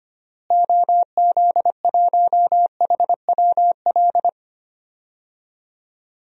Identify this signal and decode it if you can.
Morse code, heard as a keyed tone, about 25 wpm, 700 Hz. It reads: OZ1HWL
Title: DXCC-100 Call Signs - 6 Characters - Repeated 3x - 25wpm